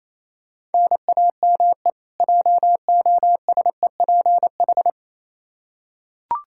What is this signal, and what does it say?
Morse code, heard as a keyed tone, about 28 wpm, 700 Hz. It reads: NAME JOSEPH
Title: Practice QSO - Exchange 1 - Beginning - 28wpm